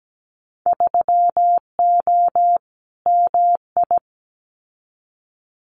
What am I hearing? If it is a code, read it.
Morse code, heard as a keyed tone, about 17 wpm, 700 Hz. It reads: 3O MI
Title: Winter Field Day Exchanges Encore 17wpm